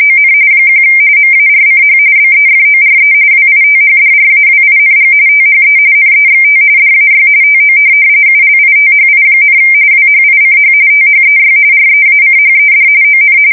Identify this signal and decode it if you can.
RTTY, about 45.45 baud. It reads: ON 1 MARCH BEFORE WITHDRAWING THE NEXT DAY TO ULITHI, WHICH THEY REACHED ON
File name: fsk170 / USS-Massachusetts